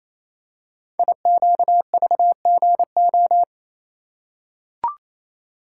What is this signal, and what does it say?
Morse code, heard as a keyed tone, about 28 wpm, 700 Hz. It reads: IQVGO
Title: Code Groups - 5 Letters - 28wpm